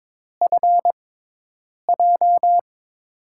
Morse code, heard as a keyed tone, about 22 wpm, spaced as for 8 wpm, 700 Hz. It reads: FJ
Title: USA Call Signs - 6 Characters - 6x Character Spacing - 22wpm